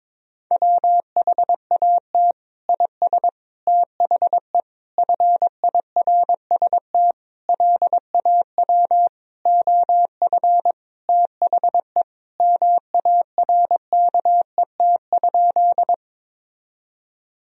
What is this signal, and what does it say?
Morse code, heard as a keyed tone, about 22 wpm, 700 Hz. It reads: WHAT IS THE FIRST LAW OF THE MARKET?
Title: Sentences from Top 1000 Words - Review Part 3 - 22wpm